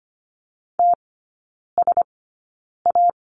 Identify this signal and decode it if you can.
Morse code, heard as a keyed tone, about 25 wpm, spaced as for 9 wpm, 700 Hz. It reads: TSA
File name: Lesson 17e - Letter u - Three Characters 25wpm